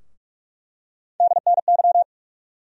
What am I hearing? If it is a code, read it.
Morse code, heard as a keyed tone, about 46 wpm, 700 Hz. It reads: DNY
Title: USA Call Sign Postfixes - 3 Letters 45wpm